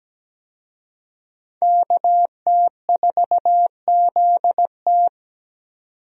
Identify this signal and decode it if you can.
Morse code, heard as a keyed tone, about 17 wpm, 700 Hz. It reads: KT4ZT
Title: USA Call Signs - 5 Characters - 17wpm